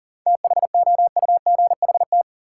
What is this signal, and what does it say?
Morse code, heard as a keyed tone, about 40 wpm, 700 Hz. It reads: THOUGHT
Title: Sets of 2 Words - New Word Drill - 40wpm